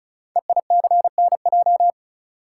Morse code, heard as a keyed tone, about 36 wpm, 690 Hz. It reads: EICNJ
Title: Code Groups - 5 Letters - 35wpm